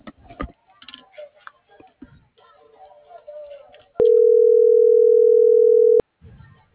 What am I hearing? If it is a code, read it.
Ringback tone (North America), 440 + 480 Hz.